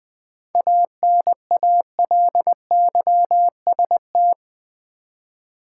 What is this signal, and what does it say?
Morse code, heard as a keyed tone, about 20 wpm, 700 Hz. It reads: ANALYST
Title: 7 Letter Words - First 3 Letters Given - 20wpm